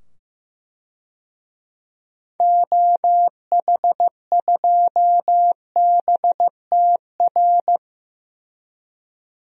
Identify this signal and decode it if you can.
Morse code, heard as a keyed tone, about 15 wpm, 700 Hz. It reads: OH2BTR
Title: DXCC-100 Call Signs - 6 Characters - 15wpm